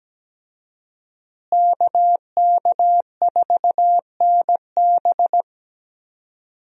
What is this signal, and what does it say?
Morse code, heard as a keyed tone, about 17 wpm, 700 Hz. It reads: KK4NB
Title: USA Call Signs - 5 Characters - Repeated 3x - 17wpm